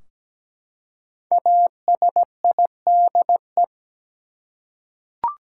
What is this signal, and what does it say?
Morse code, heard as a keyed tone, about 17 wpm, 700 Hz. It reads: ASIDE
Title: Five Letter Words - 17wpm